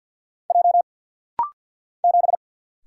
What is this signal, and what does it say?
Morse code, heard as a keyed tone, about 50 wpm, 700 Hz. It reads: J 7
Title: Single Letter-Number - Rapid-Fire 50wpm